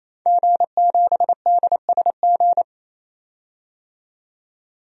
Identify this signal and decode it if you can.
Morse code, heard as a keyed tone, about 28 wpm, 700 Hz. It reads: G7DSG
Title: DXCC-100 Call Signs - 5 Characters - 28wpm